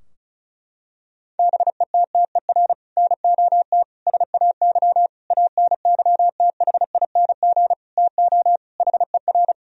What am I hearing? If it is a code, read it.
Morse code, heard as a keyed tone, about 34 wpm, 700 Hz. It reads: BETTER NOT SAY ANYTHING TO HER
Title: Sentences from Top 1000 Words - Review Part 4 - 35wpm